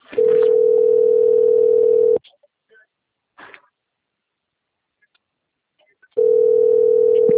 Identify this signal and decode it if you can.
Ringback tone (North America), 440 + 480 Hz.
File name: out- - - - - .39490